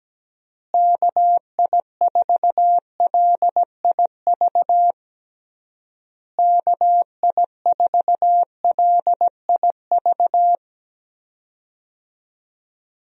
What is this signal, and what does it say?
Morse code, heard as a keyed tone, about 17 wpm, 700 Hz. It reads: KI4LIV KI4LIV
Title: USA Call Signs - 6 Characters - Repeated 3x - 17wpm